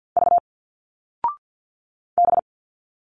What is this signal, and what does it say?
Morse code, heard as a keyed tone, about 50 wpm, 700 Hz. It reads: V B
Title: Single Letters 50wpm